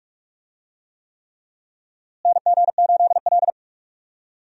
Morse code, heard as a keyed tone, about 47 wpm, 700 Hz. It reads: NG8L
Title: USA Call Signs - 4 Characters - Repeated 3x - 45wpm